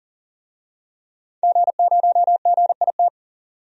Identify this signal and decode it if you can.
Morse code, heard as a keyed tone, about 40 wpm, 700 Hz. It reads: G0GIT
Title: DXCC-100 Call Signs - 5 Characters - Repeated 2x - 40wpm